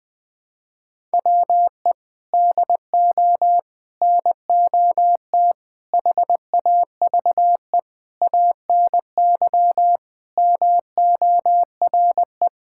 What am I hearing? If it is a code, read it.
Morse code, heard as a keyed tone, about 20 wpm, 700 Hz. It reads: WE DO NOT HAVE ANY MORE
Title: Sentences with 5-Word Phrases - Spoken Beginning - 20wpm